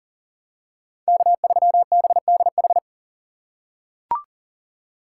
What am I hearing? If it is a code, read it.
Morse code, heard as a keyed tone, about 40 wpm, 700 Hz. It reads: K3BDH
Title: USA Call Signs - 5 Characters - 2x Character Spacing - 40wpm